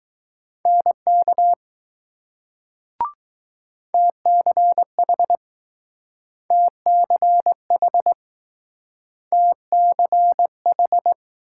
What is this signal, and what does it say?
Morse code, heard as a keyed tone, about 20 wpm, 700 Hz. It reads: NK TCH TCH TCH